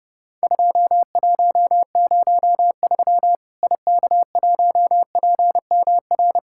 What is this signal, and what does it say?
Morse code, heard as a keyed tone, about 30 wpm, 700 Hz. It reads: 2103 IK1PMR
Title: Worked All Europe DX Contest - QTC Exchange - 30wpm